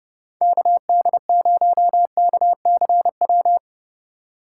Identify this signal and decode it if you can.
Morse code, heard as a keyed tone, about 35 wpm, 700 Hz. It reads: KD0KCW
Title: Speed Racing - USA 2x3 Call Signs 30wpm